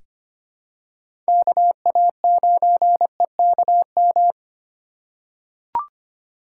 Morse code, heard as a keyed tone, about 25 wpm, 700 Hz. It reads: KA9EKM
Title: USA Call Signs - 3x Character Spacing - 25wpm